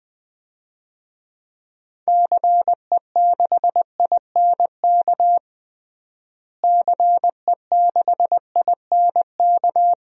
Morse code, heard as a keyed tone, about 21 wpm, 700 Hz. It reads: CE6INK CE6INK
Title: DXCC-100 Call Signs - 6 Characters - Repeated 2x - 20wpm